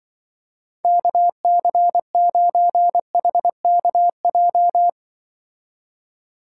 Morse code, heard as a keyed tone, about 23 wpm, 700 Hz. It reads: KC9HKJ
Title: Speed Racing - USA Call Signs 20wpm